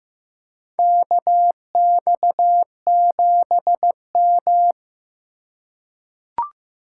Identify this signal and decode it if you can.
Morse code, heard as a keyed tone, about 15 wpm, 700 Hz. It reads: KX7M
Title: K1USN SST QSO Practice - 15wpm